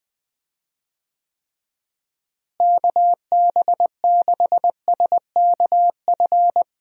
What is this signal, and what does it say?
Morse code, heard as a keyed tone, about 20 wpm, 700 Hz. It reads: KB6SKF
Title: USA Call Signs - 6 Characters - Repeated 2x - 20wpm